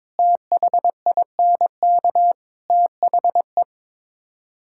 Morse code, heard as a keyed tone, about 22 wpm, 700 Hz. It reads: THINK THE
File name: practice / 22wpm / Build A Phrase - 5-Word Phrases - 22wpm